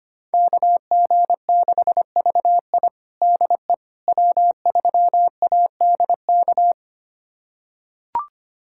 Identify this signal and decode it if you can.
Morse code, heard as a keyed tone, about 25 wpm, 700 Hz. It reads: KG6VI DE W3ADK
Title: USA Callsign DE Callsign - 5 Characters - 25wpm